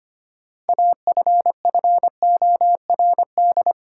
Morse code, heard as a keyed tone, about 25 wpm, 700 Hz. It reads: AFFORD
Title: Sentences from Top 1100 Words - New Word Drill Alt - 25wpm